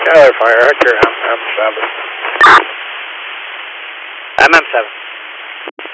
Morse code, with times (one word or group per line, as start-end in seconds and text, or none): none